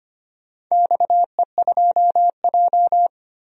Morse code, heard as a keyed tone, about 25 wpm, 700 Hz. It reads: XE2J
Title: DXCC-100 Call Signs - 4 Characters - 25wpm